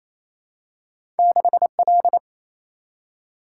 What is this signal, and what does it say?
Morse code, heard as a keyed tone, about 28 wpm, 700 Hz. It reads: 6L
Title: Single-Letter Single-Number 28wpm